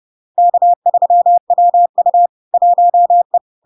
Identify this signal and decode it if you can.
Morse code, heard as a keyed tone, about 30 wpm, 700 Hz. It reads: K3WU 1E
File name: LARK-FD-2022-Log-in-CW-30wpm-700-Hz